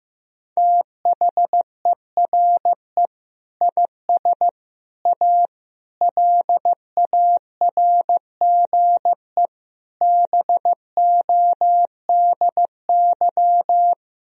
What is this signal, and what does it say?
Morse code, heard as a keyed tone, about 15 wpm, 700 Hz. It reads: THERE IS A LARGE BODY
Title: Sentences from Top 1000 Words - Review Part 4 - 15wpm